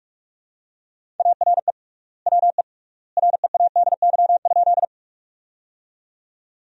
Morse code, heard as a keyed tone, about 44 wpm, 700 Hz. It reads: ARE WE READY?
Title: 3x Word-Spacing - Sentences from Top 400 Words - 45wpm